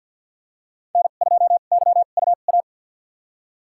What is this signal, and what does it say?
Morse code, heard as a keyed tone, about 49 wpm, spaced as for 36 wpm, 700 Hz. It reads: N2YUA
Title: USA Call Signs - 5 Characters - 2x Character Spacing - 50wpm